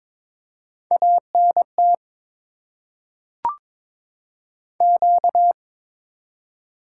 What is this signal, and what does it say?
Morse code, heard as a keyed tone, about 22 wpm, 700 Hz. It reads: ANT Q
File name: Top 20 QSO Elements - 8x Character Spacing - 22wpm